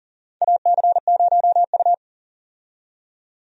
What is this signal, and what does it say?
Morse code, heard as a keyed tone, about 40 wpm, 700 Hz. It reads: AC0U
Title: USA Call Signs - 4 Characters - 40wpm